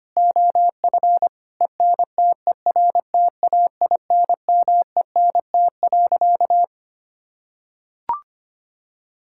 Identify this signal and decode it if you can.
Morse code, heard as a keyed tone, about 25 wpm, 700 Hz. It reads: OF ENTERTAINMENT.
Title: Sentences from Top 1000 Words - Review Part 6 - 25wpm